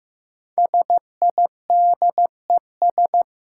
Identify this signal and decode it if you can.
Morse code, heard as a keyed tone, about 15 wpm, 700 Hz. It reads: SIDES
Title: Sets of 4 Words - New Word Drill - 15wpm